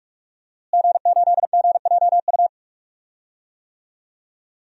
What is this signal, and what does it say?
Morse code, heard as a keyed tone, about 46 wpm, 700 Hz. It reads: G8GJU
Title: DXCC-100 Call Signs - 5 Characters - Repeated 2x - 45wpm